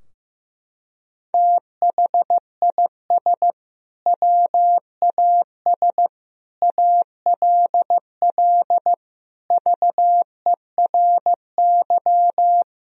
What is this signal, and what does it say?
Morse code, heard as a keyed tone, about 15 wpm, 700 Hz. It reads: THIS WAS ALL VERY
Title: Sentences from Top 1000 Words - Review Part 3 - 15wpm